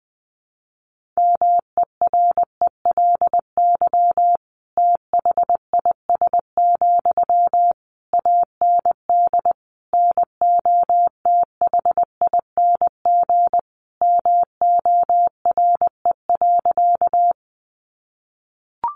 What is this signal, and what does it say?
Morse code, heard as a keyed tone, about 20 wpm, 700 Hz. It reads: MERELY THIS, AND NOTHING MORE.
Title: Sentences from Top 1400 Words 20wpm